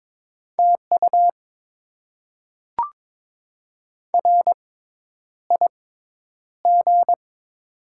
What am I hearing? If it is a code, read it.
Morse code, heard as a keyed tone, about 22 wpm, spaced as for 10 wpm, 700 Hz. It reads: TU RIG
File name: Top 20 QSO Elements - 6x Character Spacing - 22wpm